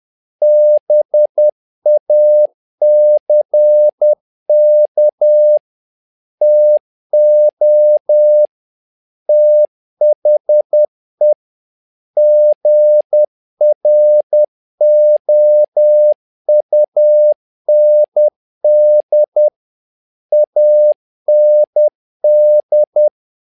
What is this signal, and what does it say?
Morse code, heard as a keyed tone, about 10 wpm, 600 Hz. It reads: BACK TO THE GROUND AND
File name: learning center / tower trouble / tower trouble 10wpm